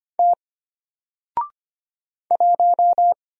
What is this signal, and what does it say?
Morse code, heard as a keyed tone, about 25 wpm, 700 Hz. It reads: T 1
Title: Single Letter-Number Alt 25wpm